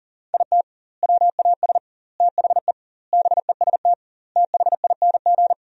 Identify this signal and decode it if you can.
Morse code, heard as a keyed tone, about 37 wpm, 700 Hz. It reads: IT WAS THE BEST THING
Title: 2x Word-Spacing - Sentences from Top 400 Words - 40wpm